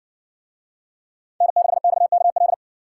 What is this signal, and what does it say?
Morse code, heard as a keyed tone, about 57 wpm, 700 Hz. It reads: N6XKL